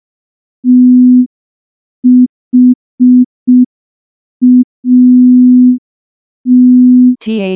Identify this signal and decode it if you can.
Morse code, heard as a keyed tone, about 5 wpm, 250 Hz. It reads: THAT
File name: morse / morse-training-05WPM